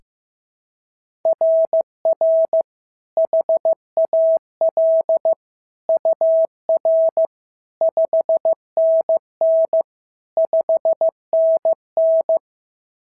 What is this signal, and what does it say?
Morse code, heard as a keyed tone, about 14 wpm, 650 Hz. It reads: RR HAL UR 5NN 5NN
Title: SOTA QSO Practice 15wpm